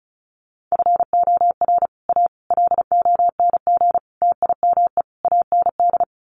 Morse code, heard as a keyed tone, about 34 wpm, 700 Hz. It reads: FOR A LONG TIME AND
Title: Sets of 5 Words Encore2 35wpm